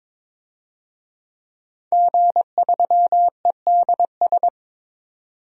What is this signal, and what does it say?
Morse code, heard as a keyed tone, about 22 wpm, 700 Hz. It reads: G3EDS
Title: DXCC-100 Call Signs - 5 Characters - 22wpm